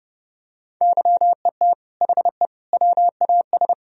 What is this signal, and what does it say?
Morse code, heard as a keyed tone, about 30 wpm, 700 Hz. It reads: YET HE WAS
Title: Sentences from Top 1000 Words - Review Part 3 - 30wpm